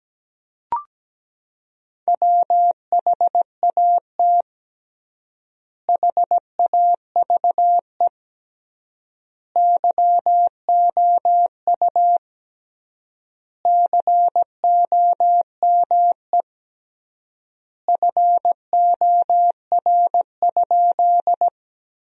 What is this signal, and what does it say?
Morse code, heard as a keyed tone, about 17 wpm, 700 Hz. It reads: WHAT HAVE YOU COME FOR?
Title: 3x Word-Spacing - Sentences from Top 100 Words - 17wpm